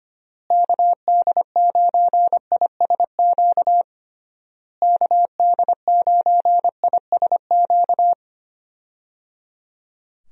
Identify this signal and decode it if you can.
Morse code, heard as a keyed tone, about 25 wpm, 700 Hz. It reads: KD9ISQ KD9ISQ
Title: USA Call Signs - 6 Characters - Repeated 3x - 25wpm